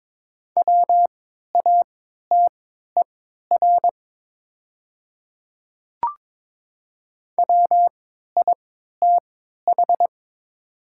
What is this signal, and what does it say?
Morse code, heard as a keyed tone, about 22 wpm, spaced as for 12 wpm, 700 Hz. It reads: WATER WITH
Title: 3x Character Spacing - Top 100 Words - 22wpm